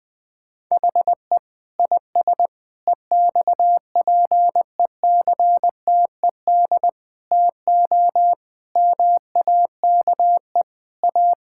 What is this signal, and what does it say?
Morse code, heard as a keyed tone, about 20 wpm, 700 Hz. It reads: HE IS EXPECTED TO MAKE A
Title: Sentences with 7-Word Phrases - Spoken Beginning - 20wpm